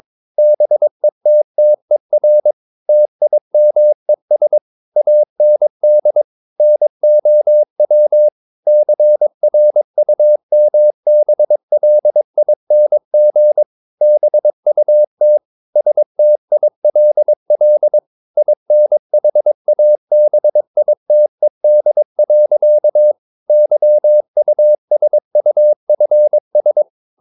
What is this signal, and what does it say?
Morse code, heard as a keyed tone, about 22 wpm, 600 Hz. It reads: BETTER TIMES AND NOW CRUMBLING BUT STILL INHABITED. YUSUFS
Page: Romanen Never af Ken Follet på engelsk.